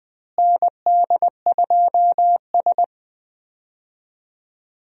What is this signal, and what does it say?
Morse code, heard as a keyed tone, about 20 wpm, 700 Hz. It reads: ND2S
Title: USA Call Signs - 4 Characters - Repeated 3x - 20wpm